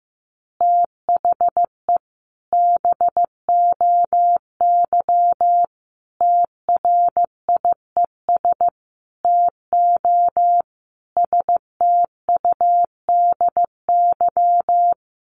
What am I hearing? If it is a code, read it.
Morse code, heard as a keyed tone, about 15 wpm, 700 Hz. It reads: THE BOY TRIES TO STUDY
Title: Additional Easy Phrases 15wpm